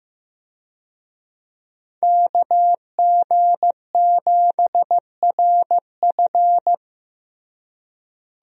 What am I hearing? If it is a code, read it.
Morse code, heard as a keyed tone, about 15 wpm, 700 Hz. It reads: KG7RF
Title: USA Call Signs - 5 Characters - Repeated 3x - 15wpm